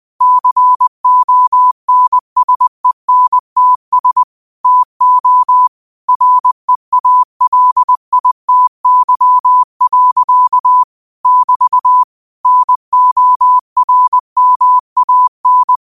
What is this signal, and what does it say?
Morse code, heard as a keyed tone, about 20 wpm, 1000 Hz. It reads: CONSENTS TO REALITY. <BT> NORMAN
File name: qotdfiles / 20wpm